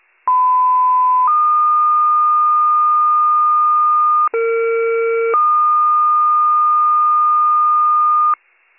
Two-tone page, heard as a one-second tone, then a three-second tone, about 1010 then 1220 Hz, about 460 then 1220 Hz. It is heard from 0.3 to 8.3 s.